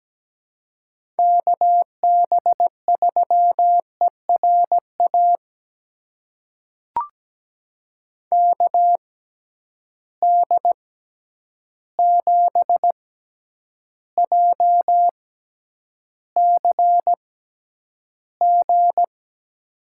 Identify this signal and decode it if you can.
Morse code, heard as a keyed tone, about 17 wpm, spaced as for 9 wpm, 700 Hz. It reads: KB3ERA KD7JCG
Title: USA Call Signs - 6x Character Spacing - 17wpm